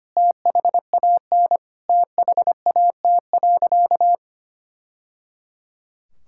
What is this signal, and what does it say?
Morse code, heard as a keyed tone, about 25 wpm, 700 Hz. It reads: THAN THAT.
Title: Sentences from Top 1000 Words - Review Part 6 - 25wpm